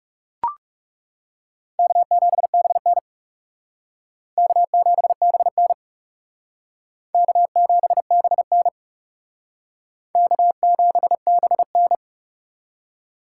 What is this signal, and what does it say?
Morse code, heard as a keyed tone, about 38 wpm, 700 Hz. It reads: K7BN K7BN K7BN K7BN